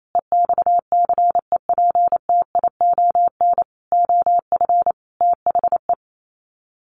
Morse code, heard as a keyed tone, about 28 wpm, 700 Hz. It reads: EXCEPTION OF THE
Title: Sets of 5 Words Encore 28wpm